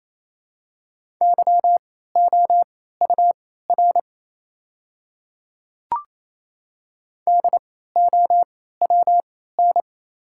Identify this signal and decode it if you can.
Morse code, heard as a keyed tone, about 28 wpm, spaced as for 16 wpm, 690 Hz. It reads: YOUR DOWN
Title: 3x Character Spacing - Top 100 Words - 28wpm